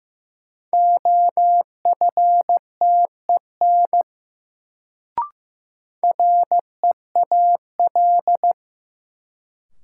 Morse code, heard as a keyed tone, about 15 wpm, 700 Hz. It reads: OFTEN REAL
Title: Sentences from Top 300 Words - New Word Drill - 15wpm